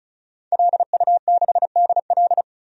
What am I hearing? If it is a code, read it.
Morse code, heard as a keyed tone, about 36 wpm, 700 Hz. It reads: LU6DL
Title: DXCC-100 Call Signs - 5 Characters - 35wpm